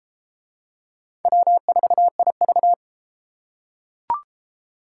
Morse code, heard as a keyed tone, about 32 wpm, 700 Hz. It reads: W4IV